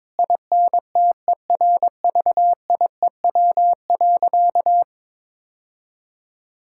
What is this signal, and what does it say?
Morse code, heard as a keyed tone, about 22 wpm, 700 Hz. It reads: INTERVIEW.
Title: Sentences from Top 1600 Words 22wpm